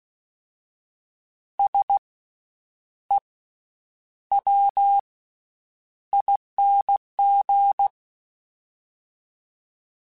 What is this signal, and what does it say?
Morse code, heard as a keyed tone, about 16 wpm, 800 Hz. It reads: S E W ING
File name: Special INGs-mp3s / Special INGs-mp3s / Slow ing @15